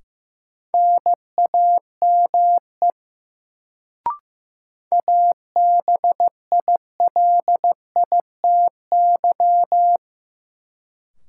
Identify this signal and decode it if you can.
Morse code, heard as a keyed tone, about 15 wpm, 700 Hz. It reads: NAME ABILITY
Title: Sets of 3 Words - New Word Drill - 15wpm